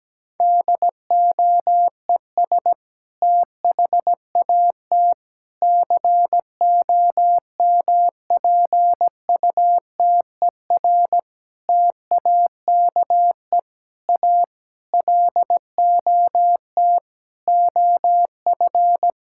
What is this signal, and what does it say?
Morse code, heard as a keyed tone, about 17 wpm, 700 Hz. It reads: DOES THAT COMPUTER TAKE A LOT OF
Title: Sentences from Top 1000 Words - Review Part 6 - 17wpm